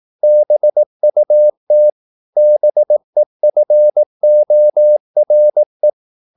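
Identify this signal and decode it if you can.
Morse code, heard as a keyed tone, about 18 wpm, 600 Hz. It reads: BUT BEFORE
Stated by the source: Morse practice at 18 wpm, English text, from CW med Gnister.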